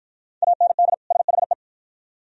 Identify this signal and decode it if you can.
Morse code, heard as a keyed tone, about 49 wpm, 700 Hz. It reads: AND SHE